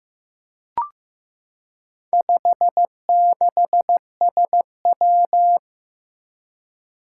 Morse code, heard as a keyed tone, about 15 wpm, 700 Hz. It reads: 56SW
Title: Four Challenging Letter-Number Combinations 15wpm